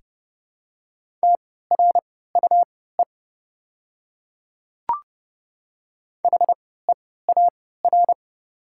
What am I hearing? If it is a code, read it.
Morse code, heard as a keyed tone, about 30 wpm, spaced as for 17 wpm, 700 Hz. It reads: TRUE HEAR
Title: 3x Character Spacing - Top 500 Words - 30wpm